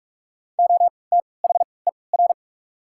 Morse code, heard as a keyed tone, about 46 wpm, spaced as for 26 wpm, 700 Hz. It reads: OTHER
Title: 3x Character Spacing - Top 300 Words - 45wpm